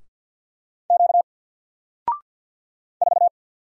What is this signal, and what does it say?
Morse code, heard as a keyed tone, about 52 wpm, 700 Hz. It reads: Q 4
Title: Single Letter-Number Alt 50wpm